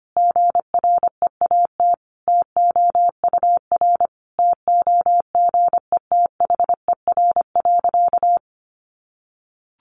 Morse code, heard as a keyed tone, about 25 wpm, 700 Hz. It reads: GREAT TOUR TOGETHER.
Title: Sentences from Top 1400 Words 25wpm